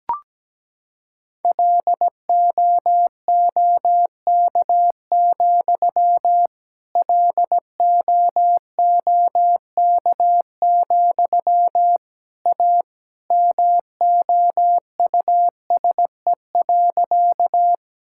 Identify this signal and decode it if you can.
Morse code, heard as a keyed tone, about 17 wpm, 700 Hz. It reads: LOOK, LOOK, A MOUSE.
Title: Sentences from Top 1000 Words - Review Part 5 - 17wpm